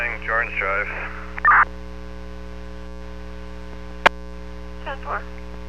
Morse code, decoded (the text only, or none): none